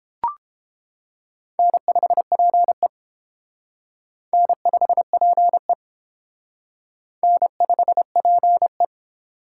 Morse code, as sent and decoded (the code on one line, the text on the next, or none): -. ..... .--. / -. ..... .--. / -. ..... .--. .
N5PE N5PE N5PE